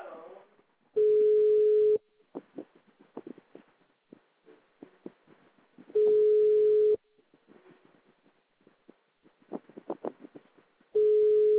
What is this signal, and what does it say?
Ringback tone (Europe, 425 Hz).